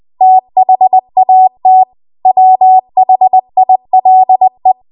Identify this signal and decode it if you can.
Morse code, heard as a keyed tone, about 20 wpm, 750 Hz. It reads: THAT WHILE
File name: Archive / 20 WPM / 20WPM